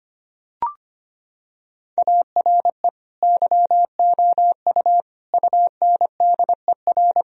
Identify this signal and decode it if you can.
Morse code, heard as a keyed tone, about 25 wpm, 700 Hz. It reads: ARE YOU UNDER
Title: Sentences from Top 1000 Words - Review Part 1 - 25wpm